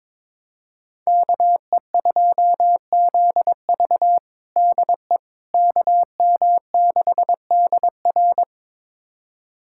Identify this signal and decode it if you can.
Morse code, heard as a keyed tone, about 22 wpm, 700 Hz. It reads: KE2ZV DE KM6DR
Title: USA Callsign DE Callsign - 5 Characters - 22wpm